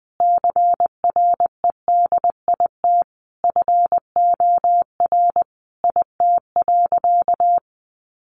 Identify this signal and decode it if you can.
Morse code, heard as a keyed tone, about 19 wpm, 700 Hz. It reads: CREDIT FOR IT.
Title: Sentences from Top 1000 Words 20wpm